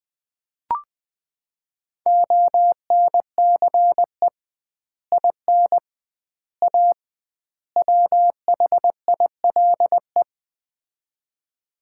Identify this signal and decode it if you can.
Morse code, heard as a keyed tone, about 20 wpm, 700 Hz. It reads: ONCE IN A WHILE
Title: 2x Word-Spacing - Sets of 4 Words - 20wpm